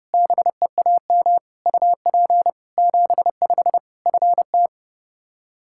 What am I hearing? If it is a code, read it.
Morse code, heard as a keyed tone, about 30 wpm, 700 Hz. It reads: BEAM UP 75 FT
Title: Practice QSO - Exchange 2 - Info - 30wpm